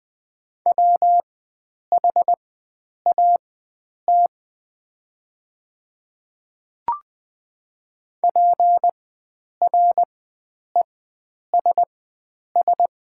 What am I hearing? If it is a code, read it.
Morse code, heard as a keyed tone, about 20 wpm, spaced as for 9 wpm, 700 Hz. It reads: WHAT PRESS
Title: 4x Character Spacing - Top 300 Words - 20wpm